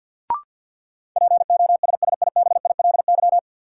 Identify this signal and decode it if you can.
Morse code, heard as a keyed tone, about 52 wpm, 700 Hz. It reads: POSSIBILY
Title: Sentences from Top 1300 Words - New Word Drill - 50wpm